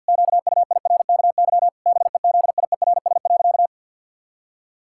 Morse code, heard as a keyed tone, about 50 wpm, 700 Hz. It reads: QUIRKY BEZIERS.
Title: Pangram Challenge v2 - 50wpm